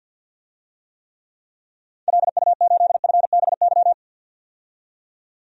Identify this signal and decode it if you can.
Morse code, heard as a keyed tone, about 51 wpm, 700 Hz. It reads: LU8FBY
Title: DXCC-100 Call Signs - 6 Characters - 50wpm